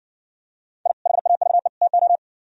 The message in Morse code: I HAVE TO

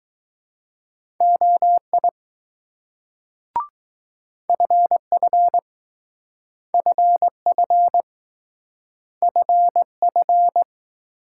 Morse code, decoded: OI FF FF FF

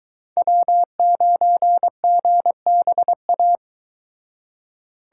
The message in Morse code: W9GBA

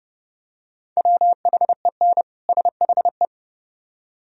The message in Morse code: WHEN SHE

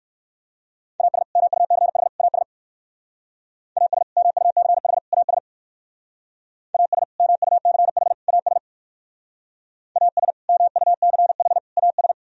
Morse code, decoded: AS MUCH AS AS MUCH AS AS MUCH AS AS MUCH AS

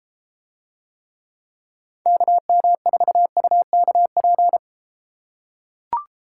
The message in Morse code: KM4UKP

